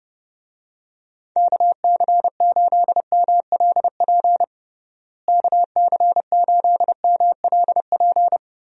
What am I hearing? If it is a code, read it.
Morse code, heard as a keyed tone, about 32 wpm, 700 Hz. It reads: KC8MLP KC8MLP
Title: USA Call Signs - 6 Characters - Repeated 3x - 30wpm